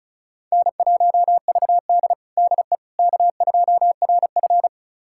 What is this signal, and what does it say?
Morse code, heard as a keyed tone, about 34 wpm, 700 Hz. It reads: N1VD DE K2RF
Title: USA Callsign DE Callsign - 4 Characters - 35wpm